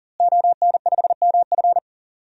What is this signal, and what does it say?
Morse code, heard as a keyed tone, about 37 wpm, 700 Hz. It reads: ON5MF
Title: DXCC-100 Call Signs - 5 Characters - Repeated 3x - 40wpm